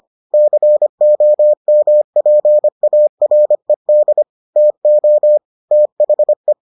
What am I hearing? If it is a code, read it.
Morse code, heard as a keyed tone, about 25 wpm, 600 Hz. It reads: COMPARED TO THE